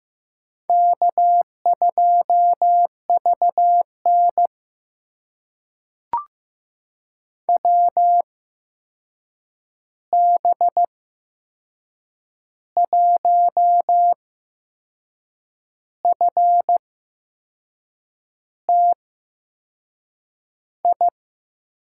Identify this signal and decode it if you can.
Morse code, heard as a keyed tone, about 15 wpm, spaced as for 6 wpm, 700 Hz. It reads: K2VN WB1FTI
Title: USA Call Signs - 8x Character Spacing - 15wpm